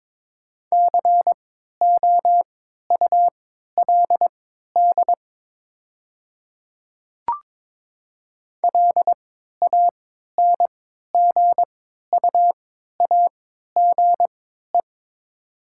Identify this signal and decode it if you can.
Morse code, heard as a keyed tone, about 22 wpm, spaced as for 12 wpm, 700 Hz. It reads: COULD LANGUAGE
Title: 3x Character Spacing - Top 500 Words - 22wpm